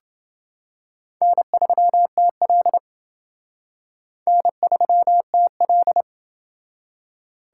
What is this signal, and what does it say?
Morse code, heard as a keyed tone, about 28 wpm, 700 Hz. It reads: N3TL N3TL